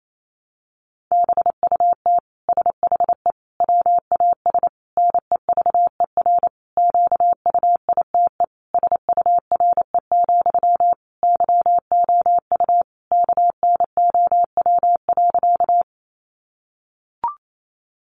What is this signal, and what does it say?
Morse code, heard as a keyed tone, about 28 wpm, 700 Hz. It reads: BUT SHE WAS NEVER QUITE SURE, YOU KNOW.
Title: Sentences from Top 700 Words 28wpm